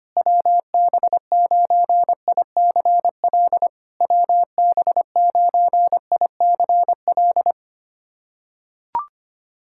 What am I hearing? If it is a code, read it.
Morse code, heard as a keyed tone, about 23 wpm, 700 Hz. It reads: WB9ICL WB9ICL
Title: SOTA QSO Practice 25wpm